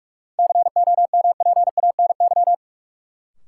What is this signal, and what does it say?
Morse code, heard as a keyed tone, about 46 wpm, 700 Hz. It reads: COMPANY
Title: Sets of 2 Words - New Word Drill - 45wpm